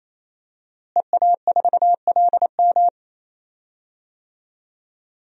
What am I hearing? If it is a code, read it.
Morse code, heard as a keyed tone, about 28 wpm, 700 Hz. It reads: EA4LM
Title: DXCC-100 Call Signs - 5 Characters - Repeated 3x - 28wpm